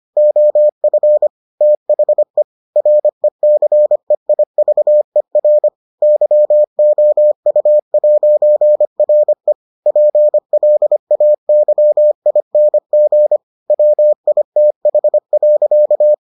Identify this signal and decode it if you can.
Morse code, heard as a keyed tone, about 25 wpm, 600 Hz. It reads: OF THE RECEIVER YOU'RE PLAYING WITH.